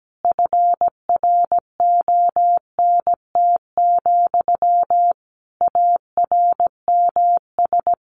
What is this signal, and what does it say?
Morse code, heard as a keyed tone, about 17 wpm, 700 Hz. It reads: FRONT, ARMS
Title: A Sound Of Thunder 17wpm